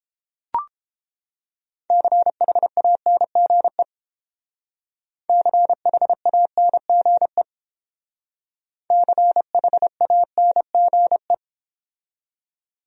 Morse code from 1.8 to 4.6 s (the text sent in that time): CHANGE